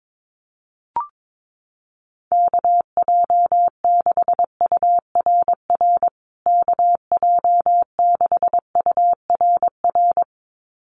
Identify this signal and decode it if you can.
Morse code, heard as a keyed tone, about 22 wpm, 700 Hz. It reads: KJ6URR KJ6URR
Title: USA Call Signs Repeated 2x 22wpm